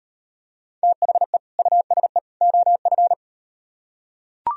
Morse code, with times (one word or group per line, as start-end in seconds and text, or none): 0.8-1.4 THE
1.6-2.2 USE
2.4-3.1 OF